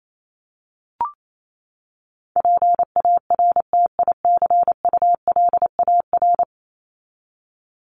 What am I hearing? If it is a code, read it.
Morse code, heard as a keyed tone, about 28 wpm, 700 Hz. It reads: PARTICULAR
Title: LY Suffix - Top 100 Words with Base Words 28wpm